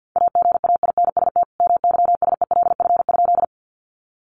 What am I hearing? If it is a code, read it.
Morse code, heard as a keyed tone, about 50 wpm, 700 Hz. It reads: AGAINST MYSELF?
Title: Sentences from Top 1300 Words 50wpm